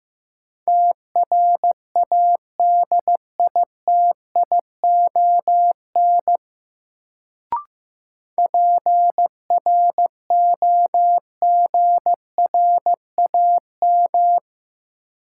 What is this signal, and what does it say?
Morse code, heard as a keyed tone, about 15 wpm, 700 Hz. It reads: TRADITION PROGRAM
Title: Sentences from Top 1500 Words - New Word Drill Alt - 15wpm